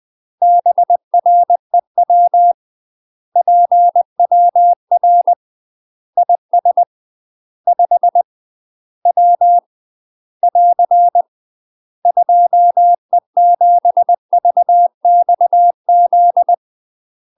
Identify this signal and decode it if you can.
Morse code, heard as a keyed tone, about 20 wpm, 700 Hz. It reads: BREW PWR IS 5 W <AR> 2E7VXZ
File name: UK MASTER QSOs TRAINING / 20WPM / UK MASTER QSOs 0063